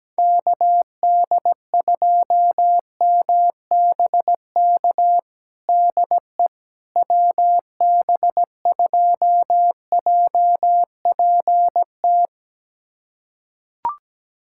Morse code, heard as a keyed tone, about 17 wpm, 700 Hz. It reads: KD2MBK DE WB2JPT
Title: USA Callsign DE Callsign 17wpm